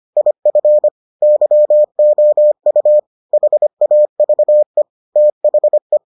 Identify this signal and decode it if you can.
Morse code, heard as a keyed tone, about 25 wpm, 600 Hz. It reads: IF YOU HAVE THE